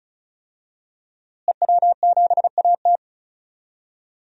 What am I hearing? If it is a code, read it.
Morse code, heard as a keyed tone, about 36 wpm, 700 Hz. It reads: EW7AT
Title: DXCC-100 Call Signs - 5 Characters - Repeated 3x - 35wpm